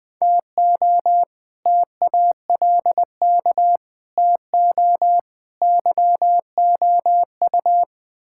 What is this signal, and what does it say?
Morse code, heard as a keyed tone, about 20 wpm, 700 Hz. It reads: TO TALK TO YOU
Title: Build A Phrase - 4-Word Phrases - 20wpm